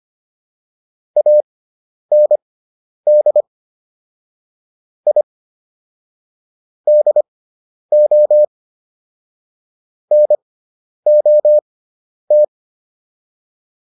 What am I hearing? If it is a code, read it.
Morse code, heard as a keyed tone, about 25 wpm, spaced as for 10 wpm, 600 Hz. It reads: AND I DO NOT